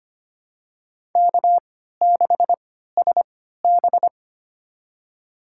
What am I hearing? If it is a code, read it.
Morse code, heard as a keyed tone, about 25 wpm, spaced as for 14 wpm, 700 Hz. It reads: K6SB
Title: USA Call Signs - 3x Character Spacing - 25wpm